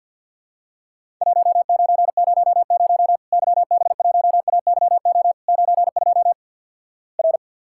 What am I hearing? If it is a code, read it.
Morse code, heard as a keyed tone, about 50 wpm, 700 Hz. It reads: 1900 YB1AYO 92 R
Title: Worked All Europe DX Contest - QTC Exchange with Repeats - 50wpm